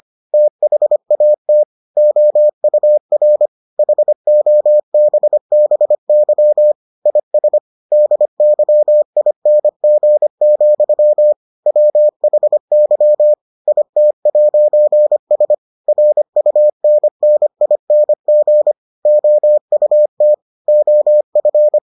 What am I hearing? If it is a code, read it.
Morse code, heard as a keyed tone, about 25 wpm, 600 Hz. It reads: THAT OUR HOBBY IS DYING, WHY IT'S RUNNING OUT OF